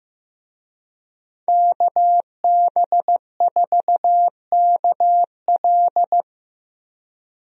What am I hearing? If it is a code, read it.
Morse code, heard as a keyed tone, about 15 wpm, 700 Hz. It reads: KB4KL
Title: USA Call Signs - 5 Characters - Repeated 3x - 15wpm